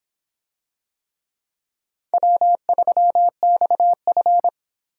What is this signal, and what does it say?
Morse code, heard as a keyed tone, about 26 wpm, 700 Hz. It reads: W3XF